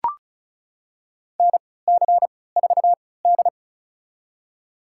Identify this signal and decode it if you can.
Morse code, heard as a keyed tone, about 36 wpm, spaced as for 20 wpm, 700 Hz. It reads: NC4D
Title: USA Call Signs - 4 Characters - 3x Character Spacing - 35wpm